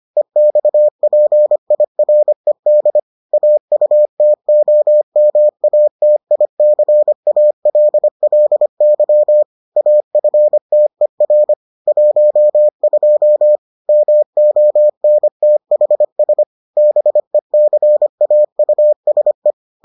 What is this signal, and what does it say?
Morse code, heard as a keyed tone, about 25 wpm, 600 Hz. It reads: EXPIRED AUTOMATICALLY AFTER 12 MONTHS BECAUSE